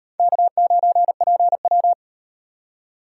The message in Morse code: K9PW

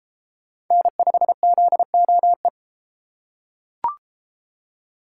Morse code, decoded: N5ZOE

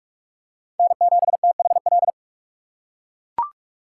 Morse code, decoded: N7THL

